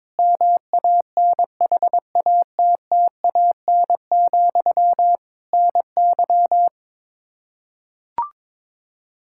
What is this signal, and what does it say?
Morse code, heard as a keyed tone, about 22 wpm, 700 Hz. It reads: MANHATTAN, NY
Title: Practice QSO - Exchange 1 - Beginning - 22wpm